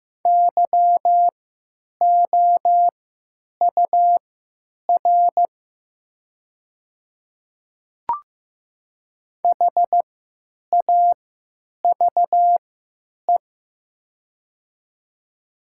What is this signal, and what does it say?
Morse code, heard as a keyed tone, about 15 wpm, spaced as for 9 wpm, 700 Hz. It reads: YOUR HAVE
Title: 3x Character Spacing - Top 100 Words - 15wpm